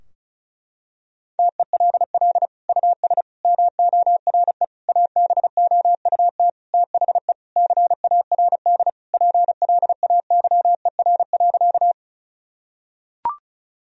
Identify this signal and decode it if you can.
Morse code, heard as a keyed tone, about 34 wpm, 700 Hz. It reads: TELL US MORE ABOUT THE CARD PLAYER.
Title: Sentences from Top 1000 Words - Review Part 4 - 35wpm